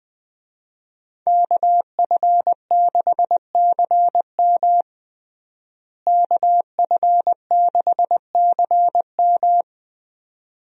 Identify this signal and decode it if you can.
Morse code, heard as a keyed tone, about 20 wpm, 700 Hz. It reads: KF6CM KF6CM
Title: USA Call Signs - 5 Characters - Repeated 2x - 20wpm